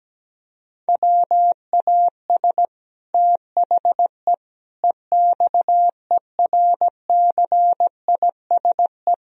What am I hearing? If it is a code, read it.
Morse code, heard as a keyed tone, about 17 wpm, 700 Hz. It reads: WAS THE EXERCISE
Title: Sentences from Top 1000 Words - Review Part 4 - 17wpm